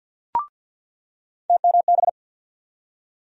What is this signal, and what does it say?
Morse code, heard as a keyed tone, about 47 wpm, 700 Hz. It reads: TMB
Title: USA Call Sign Postfixes - 3 Letters 50wpm